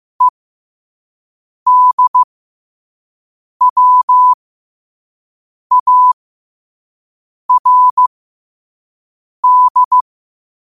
Morse code, heard as a keyed tone, about 15 wpm, spaced as for 5 wpm, 1000 Hz. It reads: EDWARD
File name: qotdfiles / 5wpm